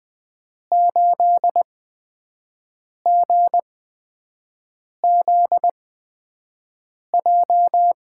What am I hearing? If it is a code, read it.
Morse code, heard as a keyed tone, about 20 wpm, spaced as for 5 wpm, 700 Hz. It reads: 8GZJ
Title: USA Call Signs - 5 Characters - 8x Character Spacing - 20wpm